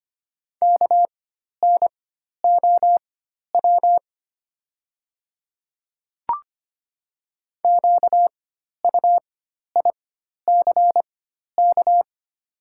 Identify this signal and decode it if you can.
Morse code, heard as a keyed tone, about 25 wpm, spaced as for 12 wpm, 700 Hz. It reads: KNOW QUICK
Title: 4x Character Spacing - Top 500 Words - 25wpm